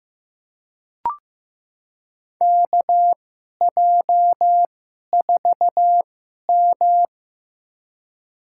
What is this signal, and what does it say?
Morse code, heard as a keyed tone, about 15 wpm, spaced as for 11 wpm, 700 Hz. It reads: KJ4M
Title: USA Call Signs - 2x Character Spacing - 15wpm